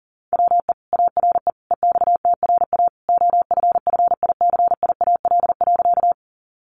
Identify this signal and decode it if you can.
Morse code, heard as a keyed tone, about 40 wpm, 700 Hz. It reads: WE ARE EXTRA OFFICIAL.
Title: Sentences from Top 1000 Words 40wpm